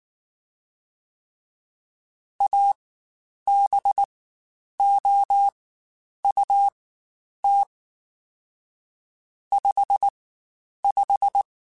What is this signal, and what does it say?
Morse code, heard as a keyed tone, about 19 wpm, spaced as for 9 wpm, 800 Hz. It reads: ABOUT 55
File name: Spaced Characters / Spaced Characters-mp3s / Weather 4 24 sp @18